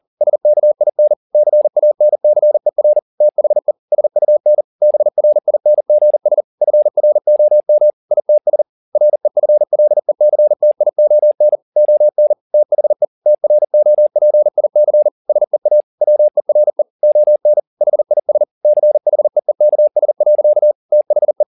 Morse code, heard as a keyed tone, about 40 wpm, 600 Hz. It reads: SKIN CANCER THE SUN BRINGS FROM ITS REFLECTION ON THE TROPIC SEA WERE ON HIS CHEEKS. THE